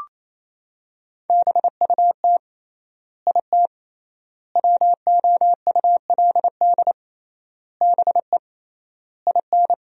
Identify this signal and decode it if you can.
Morse code, heard as a keyed tone, about 28 wpm, 700 Hz. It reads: BUT IT WOULD BE IN
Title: 3x Word-Spacing - Sentences from Top 300 Words - 28wpm